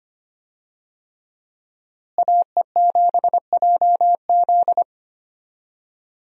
Morse code, heard as a keyed tone, about 25 wpm, 700 Hz. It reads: AE7JZ
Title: USA Call Signs - 5 Characters - Repeated 3x - 25wpm